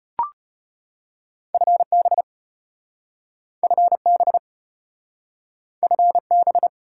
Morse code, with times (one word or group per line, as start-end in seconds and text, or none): 1.5-2.2 FB
3.6-4.4 FB
5.8-6.7 FB